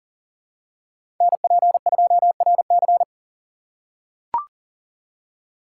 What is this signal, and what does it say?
Morse code, heard as a keyed tone, about 40 wpm, 700 Hz. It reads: NP2RC
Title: USA Call Signs - 5 Characters - 2x Character Spacing - 40wpm